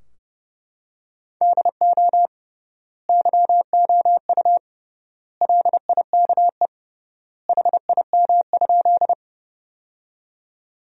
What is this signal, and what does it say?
Morse code, heard as a keyed tone, about 30 wpm, 700 Hz. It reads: DO YOU LIKE HIM?
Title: 3x Word-Spacing - Sentences from Top 100 Words - 30wpm